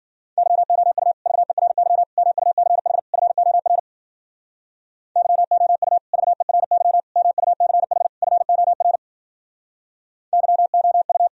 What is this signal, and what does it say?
Morse code, heard as a keyed tone, about 52 wpm, 700 Hz. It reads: YOU VERY MUCH FOR YOU VERY MUCH FOR YOU